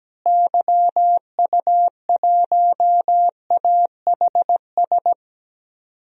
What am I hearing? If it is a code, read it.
Morse code, heard as a keyed tone, about 17 wpm, 700 Hz. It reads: YU1AHS
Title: DXCC-100 Call Signs - 6 Characters - Repeated 3x - 17wpm